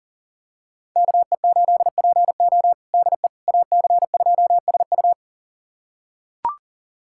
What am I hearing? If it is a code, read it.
Morse code, heard as a keyed tone, about 40 wpm, 700 Hz. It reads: KE8PO DE AC2SU
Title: USA Callsign DE Callsign - 5 Characters - 40wpm